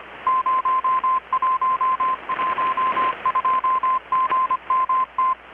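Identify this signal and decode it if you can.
Morse code, heard as a keyed tone, about 25 wpm, 1040 Hz, with radio noise. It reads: 0112GMT
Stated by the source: Morse from KPH's Night of Nights XX maritime radiotelegraphy event, received in Chiapas, Mexico about 3,800 km away on a Kenwood R-600 with a grounded 10 m long-wire antenna.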